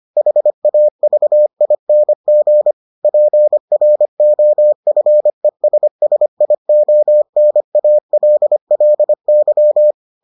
Morse code, heard as a keyed tone, about 25 wpm, 600 Hz. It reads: HAVING PROFESSIONALLY